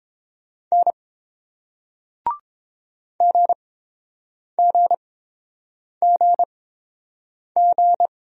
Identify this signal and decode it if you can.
Morse code, heard as a keyed tone, about 30 wpm, spaced as for 7 wpm, 700 Hz. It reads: N GGGG